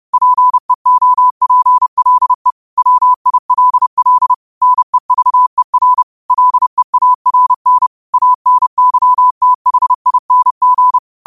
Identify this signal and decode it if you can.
Morse code, heard as a keyed tone, about 30 wpm, 1000 Hz. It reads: PEOPLE WILL NEVER LEARN ANYTHING
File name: qotdfiles / 30wpm